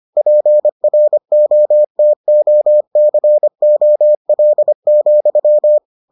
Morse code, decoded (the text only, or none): PROTOCOL,